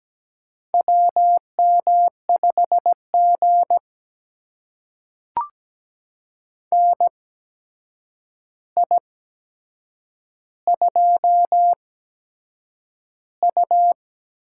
Morse code, heard as a keyed tone, about 17 wpm, spaced as for 7 wpm, 700 Hz. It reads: WM5G NI2U
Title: USA Call Signs - 4 Characters - 8x Character Spacing - 17wpm